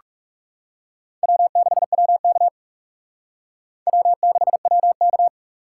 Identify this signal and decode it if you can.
Morse code, heard as a keyed tone, about 36 wpm, 700 Hz. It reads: W6WK W6WK
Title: Speed Racing - USA 1x2 Call Signs 30wpm